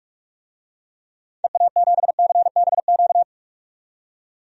Morse code, heard as a keyed tone, about 46 wpm, 700 Hz. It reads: EA7CBQ
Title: DXCC-100 Call Signs - 6 Characters - 45wpm